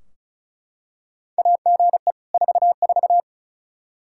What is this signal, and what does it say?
Morse code, heard as a keyed tone, about 36 wpm, 700 Hz. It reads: AGE 44
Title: Practice QSO - Exchange 3 - More Info - 35wpm